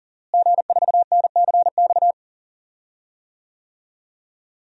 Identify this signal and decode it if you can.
Morse code, heard as a keyed tone, about 39 wpm, 700 Hz. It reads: G4NCX
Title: DXCC-100 Call Signs - 5 Characters - Repeated 2x - 40wpm